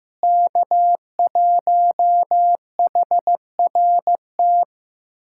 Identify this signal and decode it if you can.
Morse code, heard as a keyed tone, about 15 wpm, 700 Hz. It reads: K1HRT
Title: USA Callsign DE Callsign - 5 Characters - 15wpm